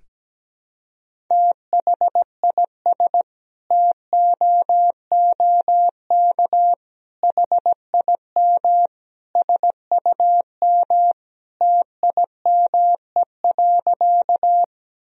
Morse code, heard as a keyed tone, about 17 wpm, 700 Hz. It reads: THIS TOOK HIM SUM TIME.
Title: Sentences from Top 300 Words and Abbreviations 17wpm